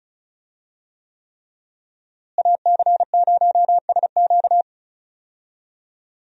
Morse code, heard as a keyed tone, about 34 wpm, 700 Hz. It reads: AC0SQ
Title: USA Call Signs - 5 Characters - Repeated 3x - 35wpm